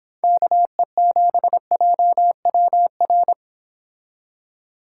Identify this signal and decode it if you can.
Morse code, heard as a keyed tone, about 26 wpm, 700 Hz. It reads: KE7JWR